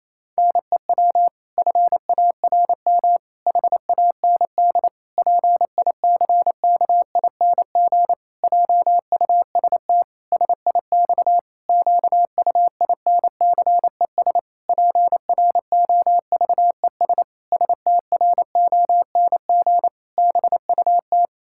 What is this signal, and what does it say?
Morse code, heard as a keyed tone, about 28 wpm, 700 Hz. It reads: NEW FARM HAND PICKING JUST SIX QUINCES PROVES STRONG BUT
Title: Pangram Challenge v2 - 28wpm